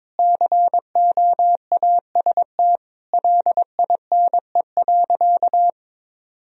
Morse code, heard as a keyed tone, about 21 wpm, 700 Hz. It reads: COAST LINE.
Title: Sentences from Top 1000 Words - Review Part 4 - 22wpm